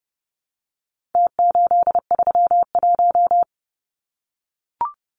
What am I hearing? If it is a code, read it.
Morse code, heard as a keyed tone, about 30 wpm, 700 Hz. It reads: T831
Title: Numbers - Contest Style - 4-Digit 30wpm